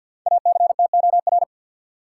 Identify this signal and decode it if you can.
Morse code, heard as a keyed tone, about 52 wpm, 700 Hz. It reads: ACTOR
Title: Sentences from Top 1500 Words - New Word Drill - 50wpm